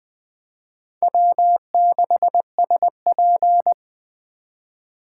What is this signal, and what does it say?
Morse code, heard as a keyed tone, about 20 wpm, 700 Hz. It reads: W6SP
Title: Speed Racing - USA 1x2 Call Signs 17wpm